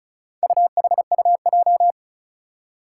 Morse code, heard as a keyed tone, about 36 wpm, 700 Hz. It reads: UHUJ
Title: Four Challenging Letter-Number Combinations 35wpm